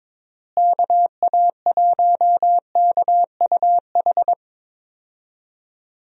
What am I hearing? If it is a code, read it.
Morse code, heard as a keyed tone, about 21 wpm, 700 Hz. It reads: KA1KUH
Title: USA Call Signs - 6 Characters - Repeated 3x - 22wpm